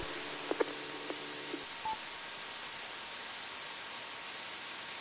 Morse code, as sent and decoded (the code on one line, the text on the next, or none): none